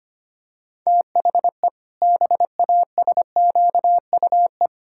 Morse code, heard as a keyed tone, about 25 wpm, 700 Hz. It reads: THE BASQUE
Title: Pangram Challenge v2 - 25wpm